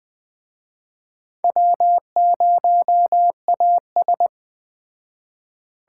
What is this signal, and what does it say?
Morse code, heard as a keyed tone, about 21 wpm, 700 Hz. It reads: W0AS
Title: Speed Racing - USA 1x2 Call Signs 17wpm